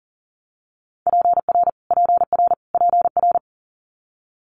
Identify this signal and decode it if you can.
Morse code, heard as a keyed tone, about 41 wpm, 700 Hz. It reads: PR PR PR
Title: Multi-letter Phonemes Repeated 3x 40wpm